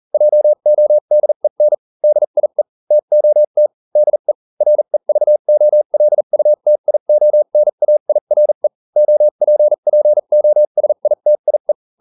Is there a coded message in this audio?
Yes: Morse code from 0.1 to 11.7 s, heard as a keyed tone, about 40 wpm, spaced as for 35 wpm, 600 Hz. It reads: JODEN DIE TOT DE REVOLUTIONAIRE OPPOSITIE